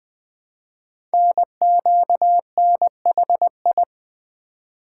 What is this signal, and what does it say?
Morse code, heard as a keyed tone, about 20 wpm, 700 Hz. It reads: NQNHI
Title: Code Groups - 5 Letters - 20wpm